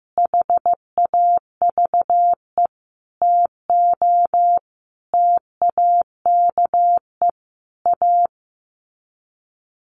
Morse code, heard as a keyed tone, about 14 wpm, 700 Hz. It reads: HAVE TO TAKE A
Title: Sets of 5 Words Encore 15wpm